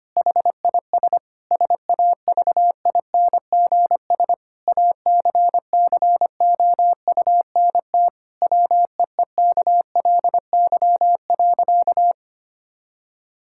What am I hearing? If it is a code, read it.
Morse code, heard as a keyed tone, about 25 wpm, 700 Hz. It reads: HIS SAVINGS ACCOUNT WEEKLY.
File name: practice / 25wpm / Difficult Sentences with Hints - 25wpm